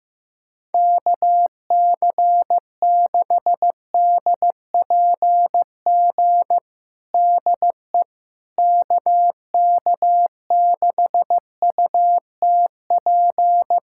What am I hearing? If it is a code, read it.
Morse code, heard as a keyed tone, about 15 wpm, 700 Hz. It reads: KC6DPG DE KK6UTP
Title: USA Callsign DE Callsign - 6 Characters - 15wpm